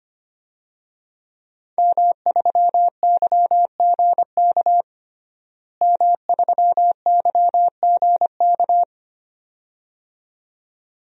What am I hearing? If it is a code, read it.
Morse code, heard as a keyed tone, about 25 wpm, 700 Hz. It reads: M3YGK M3YGK
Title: DXCC-100 Call Signs - 5 Characters - Repeated 2x - 25wpm